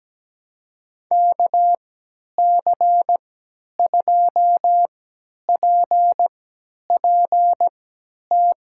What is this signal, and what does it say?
Morse code, heard as a keyed tone, about 17 wpm, spaced as for 10 wpm, 700 Hz. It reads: KC2PPT
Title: USA Call Signs - 6 Characters - 3x Character Spacing - 17wpm